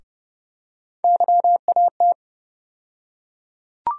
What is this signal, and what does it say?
Morse code, heard as a keyed tone, about 30 wpm, 700 Hz. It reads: YAT
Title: USA Call Sign Postfixes - 3 Letters 30wpm